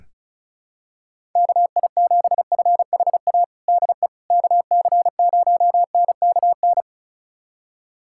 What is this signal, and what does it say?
Morse code, heard as a keyed tone, about 36 wpm, 700 Hz. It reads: KI7FHA DE KC0NKN
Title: USA Callsign DE Callsign 35wpm